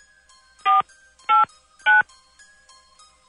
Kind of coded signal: DTMF